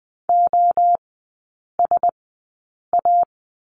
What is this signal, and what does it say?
Morse code, heard as a keyed tone, about 20 wpm, spaced as for 8 wpm, 700 Hz. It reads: OSA
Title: 041 Lesson 8e - Number 1 - Three Characters 20wpm